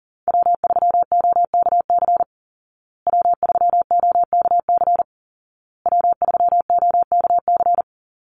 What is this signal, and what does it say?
Morse code, heard as a keyed tone, about 40 wpm, 700 Hz. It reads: W3OKC W3OKC W3OKC
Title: Call Signs Repeated 2x - 40wpm